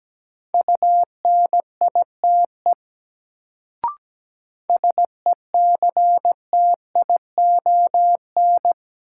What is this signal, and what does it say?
Morse code, heard as a keyed tone, about 17 wpm, 700 Hz. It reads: UNITE SECTION
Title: Sentences from Top 1000 Words - Review Part 5 - New Word Drill - 17wpm